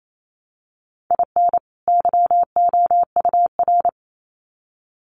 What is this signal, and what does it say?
Morse code, heard as a keyed tone, about 28 wpm, 700 Hz. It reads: IN YOUR
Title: Sets of 2 Words 28wpm